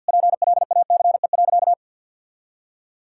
Morse code, heard as a keyed tone, about 50 wpm, 700 Hz. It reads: PLACE.
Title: Sentences from Top 1600 Words 50wpm